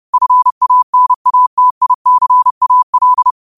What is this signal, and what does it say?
Morse code, heard as a keyed tone, about 30 wpm, 1000 Hz. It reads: FANATICAL
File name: qotdfiles / 30wpm